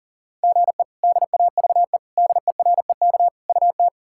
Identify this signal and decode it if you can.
Morse code, heard as a keyed tone, about 37 wpm, 700 Hz. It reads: GE DAVE DEREK UT
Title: K1USN SST QSO Practice - 40wpm